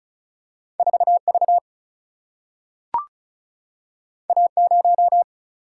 Morse code, heard as a keyed tone, about 36 wpm, 700 Hz. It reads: 4 V A 0
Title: Single-Letter Single-Number 35wpm